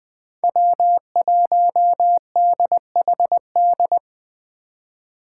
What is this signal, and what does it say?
Morse code, heard as a keyed tone, about 20 wpm, 700 Hz. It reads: W1DHD
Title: Speed Racing - USA 1x3 Call Signs 17wpm